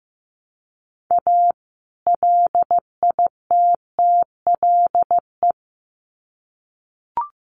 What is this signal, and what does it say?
Morse code, heard as a keyed tone, about 15 wpm, 700 Hz. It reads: A LITTLE
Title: Sets of 2 Words 15wpm